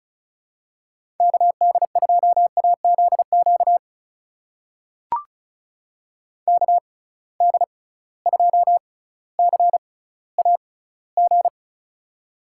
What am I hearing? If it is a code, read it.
Morse code, heard as a keyed tone, about 34 wpm, spaced as for 18 wpm, 700 Hz. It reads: KD2AZQ KD2CAG
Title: USA Call Signs - 6 Characters - 6x Character Spacing - 35wpm